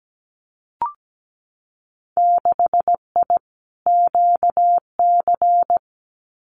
Morse code, heard as a keyed tone, about 17 wpm, 700 Hz. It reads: 6I QC
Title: Winter Field Day Exchanges Encore 17wpm